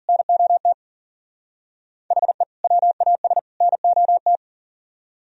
Morse code, heard as a keyed tone, about 52 wpm, 700 Hz. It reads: NOT HE WAS NOT